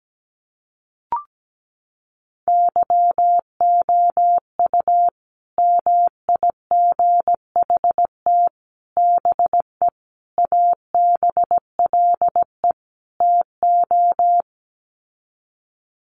Morse code, heard as a keyed tone, about 17 wpm, 700 Hz. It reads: YOU MIGHT BE ABLE TO
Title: Sets of 5 Words Encore2 17wpm